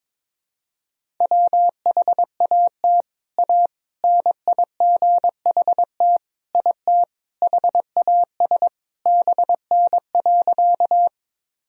Morse code, heard as a keyed tone, about 22 wpm, 700 Hz. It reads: WHAT A NIGHT IT HAS BN.
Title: Sentences from Top 300 Words and Abbreviations 22wpm